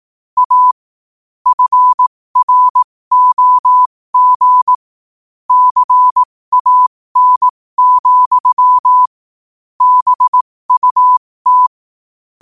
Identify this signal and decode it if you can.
Morse code, heard as a keyed tone, about 18 wpm, spaced as for 15 wpm, 1000 Hz. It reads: A FROG CAN, BUT
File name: qotdfiles / 15wpm